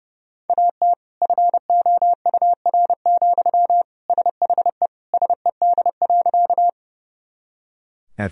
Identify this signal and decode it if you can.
Morse code, heard as a keyed tone, about 30 wpm, 700 Hz. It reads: AT FOUR, SHE SED.
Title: Sentences from Top 300 Words and Abbreviations 30wpm